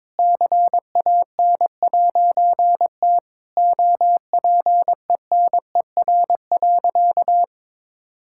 Morse code, heard as a keyed tone, about 22 wpm, 700 Hz. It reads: CAN'T OPENER.
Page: morse-code-ninja-practice-22wpm